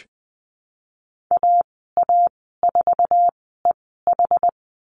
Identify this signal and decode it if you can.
Morse code, heard as a keyed tone, about 20 wpm, spaced as for 15 wpm, 700 Hz. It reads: AA4EH
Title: 069 Lesson 12g - Letter d - Call Signs 20wpm